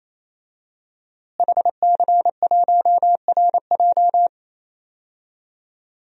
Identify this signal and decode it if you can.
Morse code, heard as a keyed tone, about 27 wpm, 700 Hz. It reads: HC1RJ
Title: DXCC-100 Call Signs - 5 Characters - Repeated 3x - 28wpm